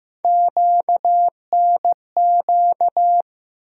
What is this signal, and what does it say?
Morse code, heard as a keyed tone, about 15 wpm, 700 Hz. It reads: QNQ
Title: USA Call Sign Postfixes - 3 Letters 15wpm